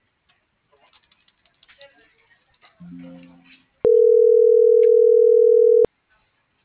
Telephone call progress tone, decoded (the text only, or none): ringback tone